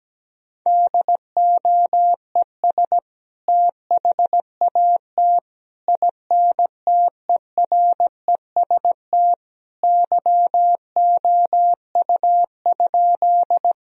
Morse code, heard as a keyed tone, about 17 wpm, 700 Hz. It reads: DOES THAT INTEREST YOU?
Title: Sentences from Top 1000 Words - Review Part 2 - 17wpm